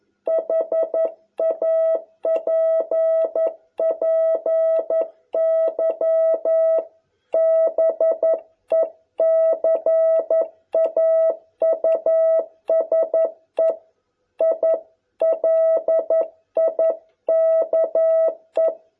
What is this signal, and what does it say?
Morse code, heard as a keyed tone, about 11 wpm, 660 Hz. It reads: HAPPY BECAUSE I LIKE